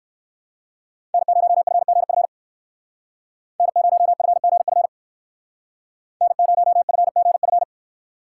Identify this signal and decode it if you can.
Morse code, heard as a keyed tone, about 58 wpm, 700 Hz. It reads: N0FGF N0FGF N0FGF